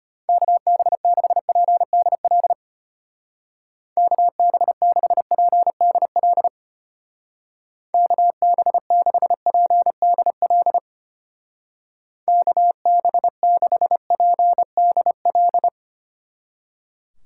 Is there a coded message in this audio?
Yes: Morse code from 0.3 to 15.7 s, heard as a keyed tone, about 32 wpm, 700 Hz. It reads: KB6PDL KB6PDL KB6PDL KB6PDL